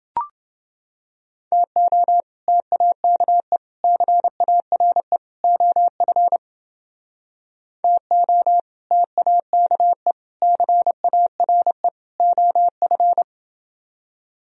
Morse code, heard as a keyed tone, about 30 wpm, 700 Hz. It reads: TO TAKE CARE OF TO TAKE CARE OF